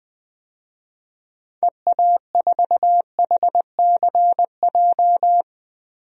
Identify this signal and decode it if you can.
Morse code, heard as a keyed tone, about 21 wpm, 700 Hz. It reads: EA4HCJ
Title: DXCC-100 Call Signs - 6 Characters - Repeated 2x - 20wpm